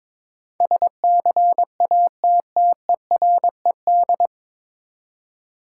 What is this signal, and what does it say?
Morse code, heard as a keyed tone, about 22 wpm, 700 Hz. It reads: SCATTERED
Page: Morse Code Ninja practice 22wpm